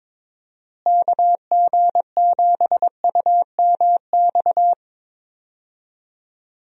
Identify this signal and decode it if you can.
Morse code, heard as a keyed tone, about 22 wpm, 700 Hz. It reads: KG7UMX
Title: USA Call Signs - 6 Characters - Repeated 3x - 22wpm